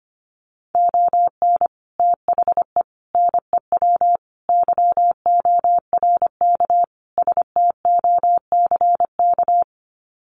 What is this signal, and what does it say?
Morse code, heard as a keyed tone, about 25 wpm, 700 Hz. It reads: ON THE NEW YORK STOCK
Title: Sets of 5 Words Encore 25wpm